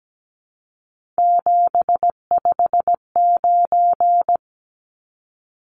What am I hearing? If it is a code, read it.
Morse code, heard as a keyed tone, about 17 wpm, 700 Hz. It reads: 759
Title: Numbers - 3-Digit 17wpm